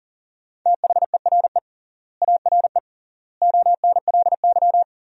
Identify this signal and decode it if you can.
Morse code, heard as a keyed tone, about 40 wpm, 700 Hz. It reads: THERE ARE ONLY
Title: 3x Word-Spacing - Sentences from Top 300 Words - 40wpm